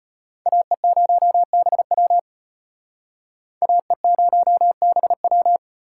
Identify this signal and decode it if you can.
Morse code, heard as a keyed tone, about 33 wpm, 700 Hz. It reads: AE0BW AE0BW